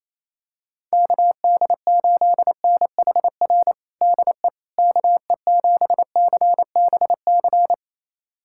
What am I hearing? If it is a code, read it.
Morse code, heard as a keyed tone, about 28 wpm, 700 Hz. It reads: KD8NHR DE KE7CBC
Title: USA Callsign DE Callsign 28wpm